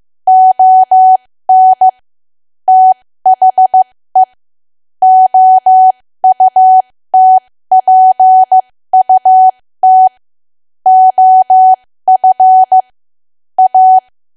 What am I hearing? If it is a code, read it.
Morse code, heard as a keyed tone, about 15 wpm, spaced as for 13 wpm, 750 Hz. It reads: ON THE OUTPUT OF A
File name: cwa / W1AW-Level2 / 13WPM